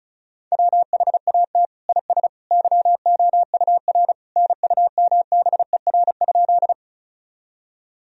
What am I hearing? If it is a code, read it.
Morse code, heard as a keyed tone, about 36 wpm, 700 Hz. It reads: WHAT IS YOUR NUMBER?
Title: Sentences from Top 1000 Words - Review Part 1 - 35wpm